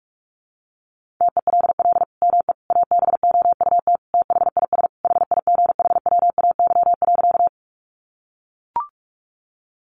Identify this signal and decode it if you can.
Morse code, heard as a keyed tone, about 44 wpm, 700 Hz. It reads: TELL ME ABOUT THIS HIGHWAY.
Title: Sentences from Top 1400 Words 45wpm